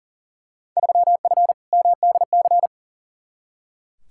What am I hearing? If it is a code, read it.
Morse code, heard as a keyed tone, about 37 wpm, 700 Hz. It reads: 3F MDC
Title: Field Day Exchanges Encore - 40wpm